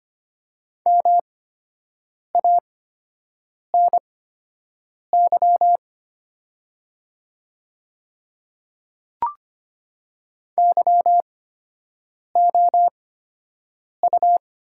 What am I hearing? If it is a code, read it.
Morse code, heard as a keyed tone, about 25 wpm, spaced as for 7 wpm, 700 Hz. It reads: MANY YOU
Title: 8x Character Spacing - Top 300 Words - 25wpm